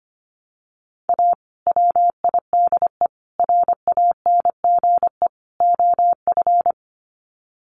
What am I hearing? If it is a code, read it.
Morse code, heard as a keyed tone, about 25 wpm, 700 Hz. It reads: A WIDE RANGE OF
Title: Sets of 4 Words 25wpm